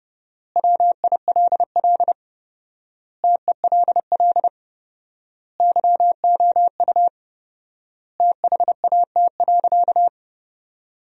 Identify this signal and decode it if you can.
Morse code, heard as a keyed tone, about 32 wpm, 700 Hz. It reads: WILL TELL YOU THAT.
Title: 4x Word-Spacing - Sentences from Top 300 Words - 30wpm